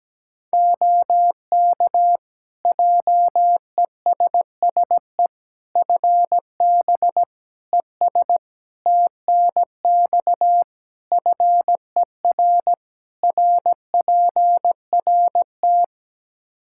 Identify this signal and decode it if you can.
Morse code, heard as a keyed tone, about 17 wpm, 700 Hz. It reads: OK JESSE FB ES TNX FER RPRT
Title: Practice QSO - Exchange 2 - Info - 17wpm